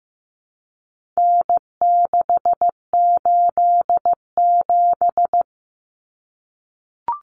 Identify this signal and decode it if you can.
Morse code, heard as a keyed tone, about 15 wpm, 700 Hz. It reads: N687
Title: Common Cut Numbers - 4-Digit 15wpm